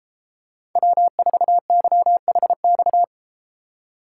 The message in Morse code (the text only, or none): W4YHX